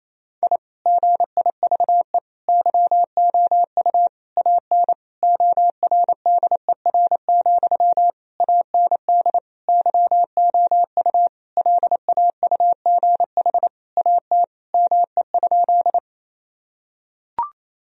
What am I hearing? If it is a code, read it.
Morse code, heard as a keyed tone, about 28 wpm, 700 Hz. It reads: I GIVE YOU AN ORDER, AND YOU LAUGH AT ME?
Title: Sentences from Top 1000 Words - Review Part 2 - 28wpm